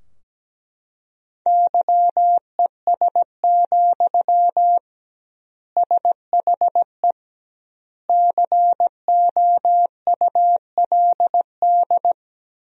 Morse code, heard as a keyed tone, about 17 wpm, 700 Hz. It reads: YES, SHE COULD
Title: 2x Word-Spacing - Sentences from Top 500 Words - 17wpm